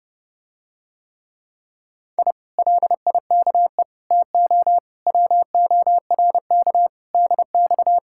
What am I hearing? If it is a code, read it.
Morse code, heard as a keyed tone, about 30 wpm, 700 Hz. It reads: I LIKE TO WORK DX